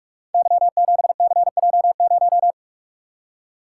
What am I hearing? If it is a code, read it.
Morse code, heard as a keyed tone, about 44 wpm, 700 Hz. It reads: Y7CJ0
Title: Code Groups - 5 Letter:Numbers - 45wpm